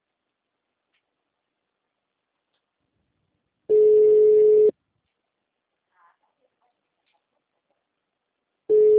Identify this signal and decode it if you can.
Ringback tone (Europe, 425 Hz).
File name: out- -2027- - - .21995